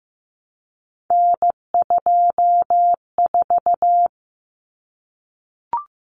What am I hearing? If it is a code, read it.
Morse code, heard as a keyed tone, about 15 wpm, 700 Hz. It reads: N24
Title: Common Cut Numbers - 3-Digit 15wpm